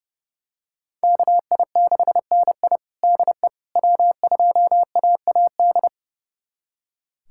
Morse code, heard as a keyed tone, about 32 wpm, 700 Hz. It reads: KI6NI DE W2AAD
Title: USA Callsign DE Callsign - 5 Characters - 30wpm